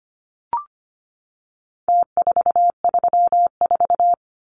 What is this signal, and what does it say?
Morse code, heard as a keyed tone, about 25 wpm, 700 Hz. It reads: T434
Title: Numbers - Contest Style - 4-Digit 25wpm